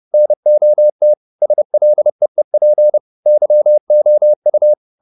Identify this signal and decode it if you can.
Morse code, heard as a keyed tone, about 30 wpm, 600 Hz. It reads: NOT SLEEP YOU